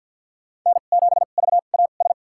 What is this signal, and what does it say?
Morse code, heard as a keyed tone, about 46 wpm, spaced as for 34 wpm, 700 Hz. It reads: N7VAS
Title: USA Call Signs - 5 Characters - 2x Character Spacing - 50wpm